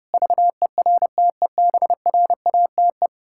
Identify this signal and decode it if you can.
Morse code, heard as a keyed tone, about 32 wpm, 700 Hz. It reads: VERTEBRATE
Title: Pangram Challenge v2 - 30wpm